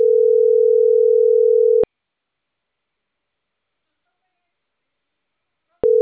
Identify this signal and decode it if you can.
Ringback tone (North America), 440 + 480 Hz.